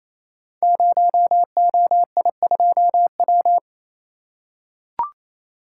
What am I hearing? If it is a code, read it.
Morse code, heard as a keyed tone, about 29 wpm, 700 Hz. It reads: 0OI2W
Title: Code Groups - 5 Letter:Numbers - 28wpm